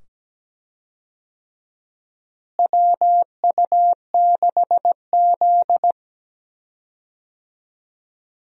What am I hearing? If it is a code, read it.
Morse code, heard as a keyed tone, about 17 wpm, 700 Hz. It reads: WU6Z
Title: USA Call Signs - 4 Characters - Repeated 2x - 17wpm